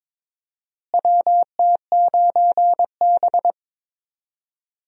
Morse code, heard as a keyed tone, about 21 wpm, 700 Hz. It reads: WT9B
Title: USA Call Signs - 4 Characters - Repeated 3x - 22wpm